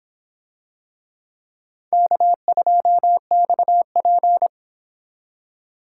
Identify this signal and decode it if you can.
Morse code, heard as a keyed tone, about 26 wpm, 700 Hz. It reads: K2XP